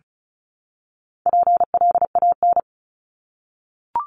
Morse code, heard as a keyed tone, about 34 wpm, 700 Hz. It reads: PLAN
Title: ING Suffix - Top 100 Words with Base Words 35wpm